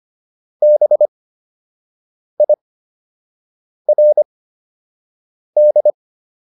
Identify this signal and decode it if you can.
Morse code, heard as a keyed tone, about 25 wpm, spaced as for 6 wpm, 600 Hz. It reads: BIRD